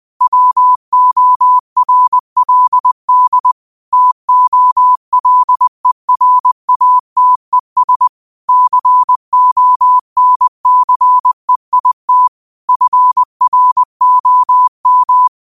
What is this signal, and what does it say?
Morse code, heard as a keyed tone, about 20 wpm, 1000 Hz. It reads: WORLD TOLERATES CONCEIT FROM
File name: qotdfiles / 20wpm